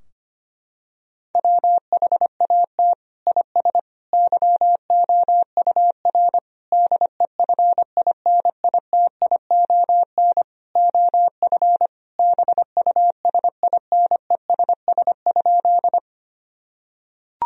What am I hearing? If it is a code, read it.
Morse code, heard as a keyed tone, about 25 wpm, 700 Hz. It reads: WHAT IS YOUR DEFINITION OF BUSINESS?
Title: Sentences from Top 1000 Words - Review Part 4 - 25wpm